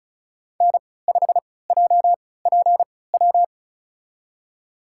Morse code, heard as a keyed tone, about 36 wpm, spaced as for 20 wpm, 700 Hz. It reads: N5JPW
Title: USA Call Signs - 3x Character Spacing - 35wpm